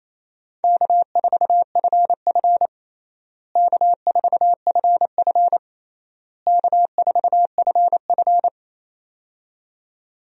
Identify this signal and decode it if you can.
Morse code, heard as a keyed tone, about 27 wpm, 700 Hz. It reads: K4FF K4FF K4FF
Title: USA Call Signs - 4 Characters - Repeated 3x - 28wpm